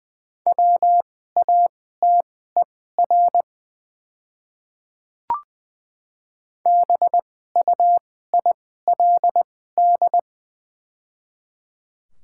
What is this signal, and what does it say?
Morse code, heard as a keyed tone, about 20 wpm, spaced as for 14 wpm, 700 Hz. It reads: WATER BUILD
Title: 2x Character Spacing - Top 200 Words - 20wpm